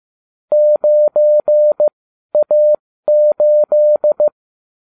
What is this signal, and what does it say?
Morse code, heard as a keyed tone, about 15 wpm, spaced as for 12 wpm, 600 Hz. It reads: 9A8